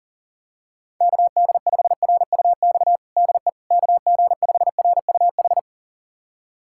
Morse code, heard as a keyed tone, about 40 wpm, 700 Hz. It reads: KD5RUX DE KG5RUH
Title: USA Callsign DE Callsign - 6 Characters - 40wpm